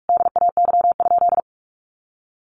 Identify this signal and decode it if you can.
Morse code, heard as a keyed tone, about 44 wpm, 700 Hz. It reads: DAY?
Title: Sentences from Top 1400 Words 45wpm